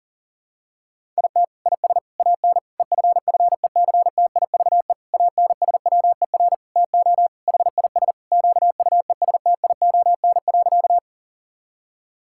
Morse code, heard as a keyed tone, about 40 wpm, 700 Hz. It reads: IT IS AN EFFECTIVE ANSWER TO HIS QUESTION.
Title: Sentences from Top 1000 Words - Review Part 6 - 40wpm